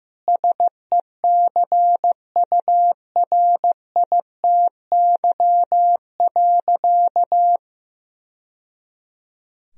Morse code, heard as a keyed tone, about 16 wpm, 700 Hz. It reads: SECURITY.
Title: Sentences from Top 1000 Words - Review Part 4 - 15wpm